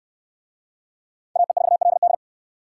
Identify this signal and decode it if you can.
Morse code, heard as a keyed tone, about 66 wpm, 700 Hz. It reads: AE4LD